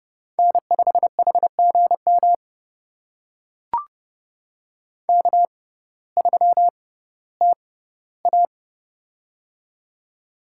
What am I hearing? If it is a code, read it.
Morse code, heard as a keyed tone, about 30 wpm, spaced as for 16 wpm, 700 Hz. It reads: N5HGM K3TA
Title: USA Call Signs - 6x Character Spacing - 30wpm